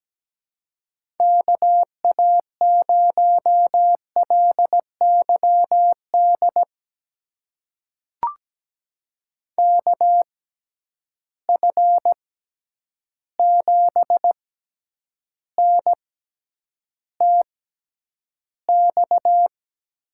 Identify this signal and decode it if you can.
Morse code, heard as a keyed tone, about 17 wpm, spaced as for 9 wpm, 700 Hz. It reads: KA0LYD KF7NTX
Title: USA Call Signs - 6 Characters - 6x Character Spacing - 17wpm